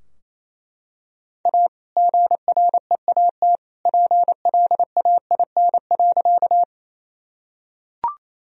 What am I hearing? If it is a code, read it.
Morse code, heard as a keyed tone, about 28 wpm, 700 Hz. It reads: A GREAT PLAIN.
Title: Sentences from Top 1000 Words - Review Part 2 - 28wpm